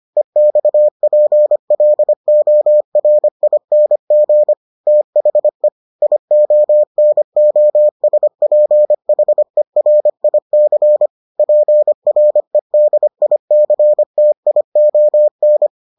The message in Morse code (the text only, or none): EXPLORING THE IONOSPHERIC PREDICTION